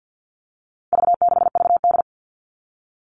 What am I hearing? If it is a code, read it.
Morse code, heard as a keyed tone, about 50 wpm, 700 Hz. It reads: V6VD
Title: All Cut Numbers - 4-Digit 50wpm